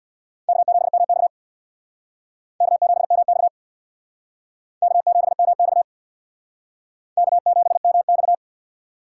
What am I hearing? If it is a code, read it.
Morse code, heard as a keyed tone, about 64 wpm, 700 Hz. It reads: K7MX K7MX K7MX K7MX